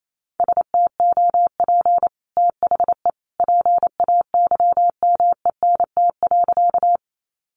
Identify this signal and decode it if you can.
Morse code, heard as a keyed tone, about 28 wpm, 700 Hz. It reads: STOP THE PAYMENT.
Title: Sentences from Top 1200 Words 28wpm